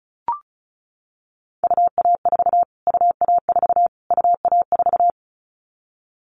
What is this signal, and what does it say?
Morse code, heard as a keyed tone, about 34 wpm, 700 Hz. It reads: UA4 UA4 UA4
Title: Callsign Prefixes for Top 25 DXCC - Repeated 3x - 35wpm